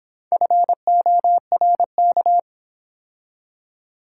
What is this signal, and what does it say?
Morse code, heard as a keyed tone, about 26 wpm, 700 Hz. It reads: FORK